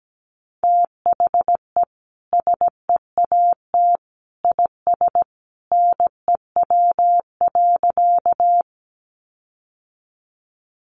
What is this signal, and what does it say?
Morse code, heard as a keyed tone, about 17 wpm, 700 Hz. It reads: THE SEAT IS NEW.
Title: Sentences from Top 1500 Words 17wpm